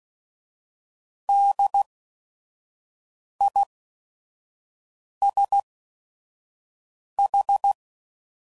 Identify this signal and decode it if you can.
Morse code, heard as a keyed tone, about 16 wpm, spaced as for 5 wpm, 800 Hz. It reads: DISH
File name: Prefixes / By First Letter / Ds